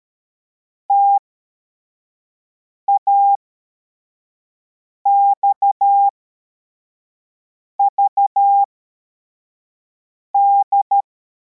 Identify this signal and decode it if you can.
Morse code, heard as a keyed tone, about 13 wpm, spaced as for 4 wpm, 800 Hz. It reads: TAXVD
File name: formation / cw / Text55